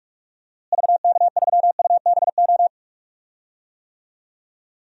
Morse code, heard as a keyed tone, about 46 wpm, 700 Hz. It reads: VK3UBO
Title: DXCC-100 Call Signs - 6 Characters - Repeated 2x - 45wpm